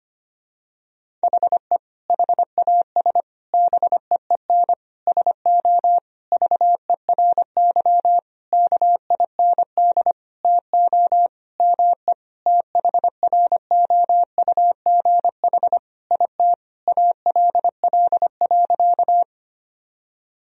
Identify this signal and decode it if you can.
Morse code, heard as a keyed tone, about 25 wpm, 700 Hz. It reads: HE HAS BEEN SO VERY KIND TO ME THROUGH IT ALL.
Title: Sentences from Top 1000 Words - Review Part 1 - 25wpm